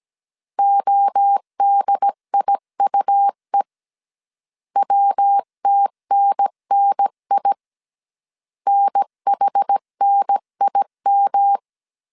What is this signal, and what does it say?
Morse code, heard as a keyed tone, about 17 wpm, 800 Hz. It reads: ODIUE WTNNI NHNIM
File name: CW / DET / 1 3 1500C S A 16wpm 2141